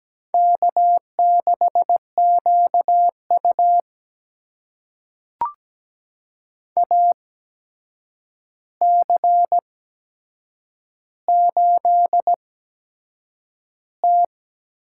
Morse code, heard as a keyed tone, about 17 wpm, spaced as for 7 wpm, 700 Hz. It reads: K6QU AC8T
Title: USA Call Signs - 8x Character Spacing - 17wpm